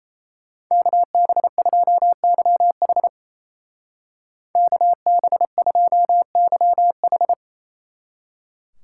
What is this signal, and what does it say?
Morse code, heard as a keyed tone, about 35 wpm, 700 Hz. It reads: KB2YH KB2YH